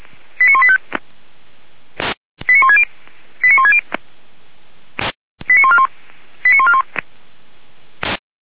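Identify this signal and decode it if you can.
Five-tone selective call (ZVEI), heard as quick runs of five tones, tones of about 70 ms. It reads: 89185 89160 89160 89141 89141
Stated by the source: ZVEI five-tone selective call received on SDR.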